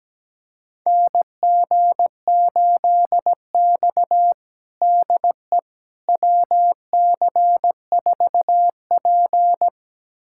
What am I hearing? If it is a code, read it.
Morse code, heard as a keyed tone, about 17 wpm, 700 Hz. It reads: NG8X DE WC4P
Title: USA Callsign DE Callsign - 4 Characters - 17wpm